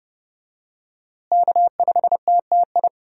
Morse code, heard as a keyed tone, about 30 wpm, 700 Hz. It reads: K5TTI
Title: USA Call Signs - 5 Characters - Repeated 3x - 30wpm